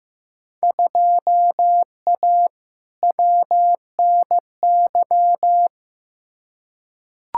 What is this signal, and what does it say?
Morse code, heard as a keyed tone, about 15 wpm, 700 Hz. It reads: 2A WNY
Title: Field Day Exchanges - 15wpm